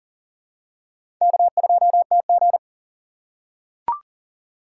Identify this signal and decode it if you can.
Morse code, heard as a keyed tone, about 40 wpm, 700 Hz. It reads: K2TG
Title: USA Call Signs - 4 Characters - 4x Character Spacing - 40wpm